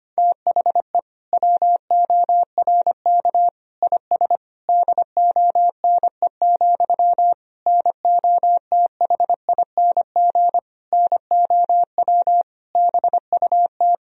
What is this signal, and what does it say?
Morse code, heard as a keyed tone, about 25 wpm, 700 Hz. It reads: THE WORK IS DONE, NOTHING NOW BUT
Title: Sentences from Top 1000 Words - Review Part 4 - 25wpm